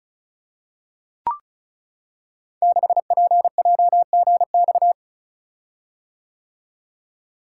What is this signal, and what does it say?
Morse code, heard as a keyed tone, about 36 wpm, 690 Hz. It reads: 6PJGX
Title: Code Groups - 5 Letter:Numbers - 35wpm